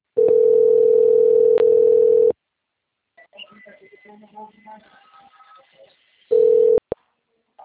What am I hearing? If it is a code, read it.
Ringback tone (North America), 440 + 480 Hz, with speech.
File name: out- - - - - .33456